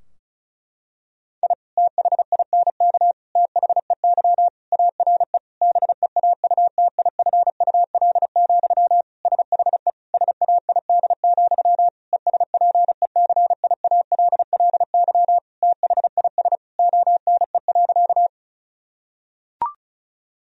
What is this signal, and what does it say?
Morse code, heard as a keyed tone, about 36 wpm, 700 Hz. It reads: I THINK THEY ARE BEAUTIFUL, SHE SAID, ESPECIALLY THIS ONE.
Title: Sentences from Top 1000 Words - Review Part 4 - 35wpm